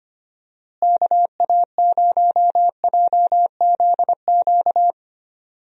Morse code, heard as a keyed tone, about 25 wpm, 700 Hz. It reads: KA0JZQ
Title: USA Call Signs - 6 Characters - Repeated 3x - 25wpm